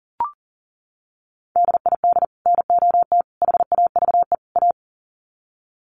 Morse code, heard as a keyed tone, about 40 wpm, 700 Hz. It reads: DID NOT HAVE A
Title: Sets of 4 Words 40wpm